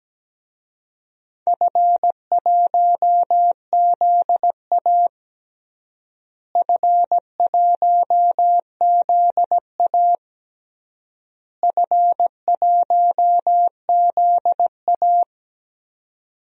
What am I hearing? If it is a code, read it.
Morse code, heard as a keyed tone, about 17 wpm, 700 Hz. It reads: F1ZA F1ZA F1ZA
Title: DXCC-100 Call Signs - 4 Characters - Repeated 3x - 17wpm